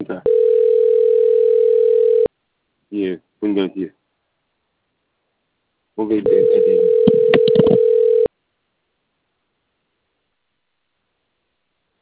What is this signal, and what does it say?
Ringback tone (North America), 440 + 480 Hz, with speech.
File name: out- - - - - .57785